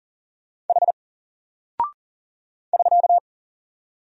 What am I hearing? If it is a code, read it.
Morse code, heard as a keyed tone, about 38 wpm, 700 Hz. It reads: H <SK>
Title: Single Character (Letter:Number:Punctuation:Prosign) Alt 40wpm